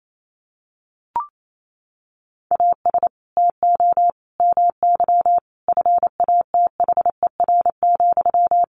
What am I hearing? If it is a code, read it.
Morse code, heard as a keyed tone, about 28 wpm, 700 Hz. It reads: AS TO MY FATHER,
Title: Sentences from Top 600 Words 28wpm